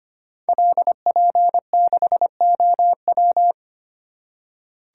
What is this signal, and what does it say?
Morse code, heard as a keyed tone, about 25 wpm, 700 Hz. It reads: LP6OW
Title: Code Groups - 5 Letter:Numbers - 25wpm